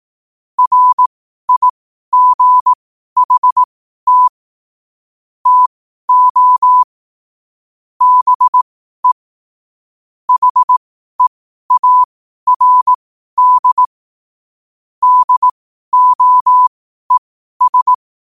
Morse code, heard as a keyed tone, about 18 wpm, spaced as for 12 wpm, 1000 Hz. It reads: RIGHT TO BE HEARD DOES
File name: qotdfiles / 12wpm